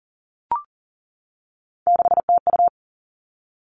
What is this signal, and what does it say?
Morse code, heard as a keyed tone, about 41 wpm, 700 Hz. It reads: 6TU
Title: Three Challenging Letter-Number Combinations 40wpm